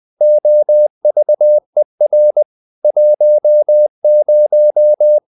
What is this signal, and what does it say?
Morse code, heard as a keyed tone, about 20 wpm, 600 Hz. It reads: OVER 10
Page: Ferie 20 wpm